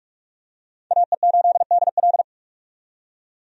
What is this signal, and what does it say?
Morse code, heard as a keyed tone, about 46 wpm, 700 Hz. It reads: AE8DL
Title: USA Call Signs - 5 Characters - 45wpm